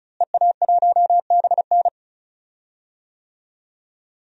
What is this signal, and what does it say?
Morse code, heard as a keyed tone, about 36 wpm, 700 Hz. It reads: EA1BN
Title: DXCC-100 Call Signs - 5 Characters - Repeated 2x - 35wpm